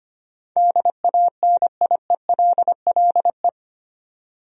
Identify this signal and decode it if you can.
Morse code, heard as a keyed tone, about 25 wpm, 700 Hz. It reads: DANIELLE
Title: USA Common Names - 25wpm